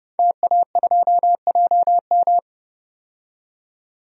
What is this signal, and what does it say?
Morse code, heard as a keyed tone, about 30 wpm, 700 Hz. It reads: TA2JM
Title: DXCC-100 Call Signs - 5 Characters - 30wpm